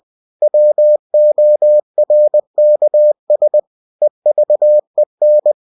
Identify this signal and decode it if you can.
Morse code, heard as a keyed tone, about 20 wpm, 600 Hz. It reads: WORKS EVEN
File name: cw-books / winnie-the-pooh / winnie-20wpm-page-0113